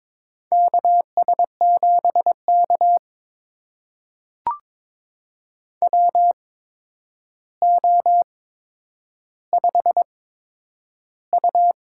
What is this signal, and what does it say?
Morse code, heard as a keyed tone, about 22 wpm, spaced as for 9 wpm, 700 Hz. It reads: KS7K WO5U
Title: USA Call Signs - 4 Characters - 8x Character Spacing - 22wpm